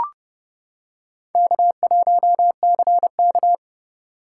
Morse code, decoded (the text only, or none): K1CK